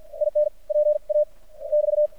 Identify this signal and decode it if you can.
Morse code, heard as a keyed tone, about 53 wpm, spaced as for 45 wpm, 600 Hz. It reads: IN PA <BT>
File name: archived / CRSnet